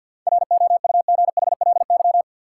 Morse code, heard as a keyed tone, about 50 wpm, 700 Hz. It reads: ROUGHLY
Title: Seven Letter Words - 50wpm